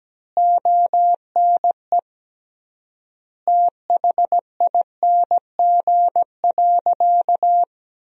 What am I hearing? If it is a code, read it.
Morse code, heard as a keyed tone, about 17 wpm, 700 Hz. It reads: ONE THING.
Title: 3x Word-Spacing - Sentences from Top 300 Words - 17wpm